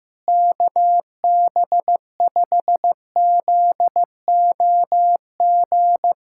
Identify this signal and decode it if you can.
Morse code, heard as a keyed tone, about 15 wpm, 700 Hz. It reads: KB5ZOG
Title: USA Callsign DE Callsign - 6 Characters - 15wpm